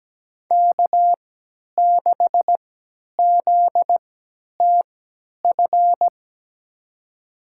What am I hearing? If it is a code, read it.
Morse code, heard as a keyed tone, about 17 wpm, spaced as for 10 wpm, 700 Hz. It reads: K6ZTF
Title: USA Call Signs - 5 Characters - 3x Character Spacing - 17wpm